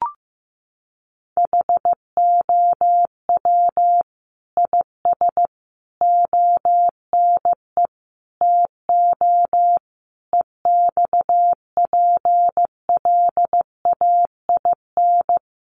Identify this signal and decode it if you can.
Morse code, heard as a keyed tone, about 15 wpm, 700 Hz. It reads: HOW IS ONE TO EXPLAIN
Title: Sentences from Top 1100 Words 15wpm